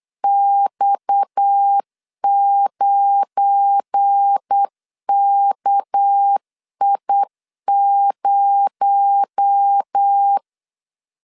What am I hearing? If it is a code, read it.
Morse code, heard as a keyed tone, about 8 wpm, 800 Hz. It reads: X9KI0
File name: LEZIONI / 7 GF9Z / 7 GF9 8wpm 3545